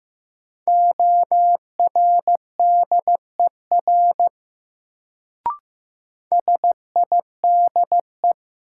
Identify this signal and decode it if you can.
Morse code, heard as a keyed tone, about 15 wpm, 700 Hz. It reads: ORDER SIDE
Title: Sets of 2 Words - New Word Drill - 15wpm